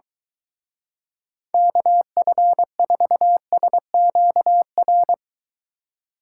Morse code, heard as a keyed tone, about 23 wpm, 700 Hz. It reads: KF4SQR